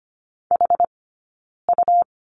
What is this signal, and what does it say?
Morse code, heard as a keyed tone, about 25 wpm, 700 Hz. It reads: H U
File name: Lesson 17d - Letter u - Two Characters 25wpm